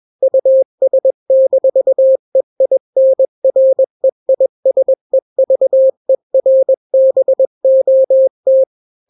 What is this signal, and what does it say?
Morse code, heard as a keyed tone, about 21 wpm, 530 Hz. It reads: US-EINREISEVERBOT
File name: daily / 20wpm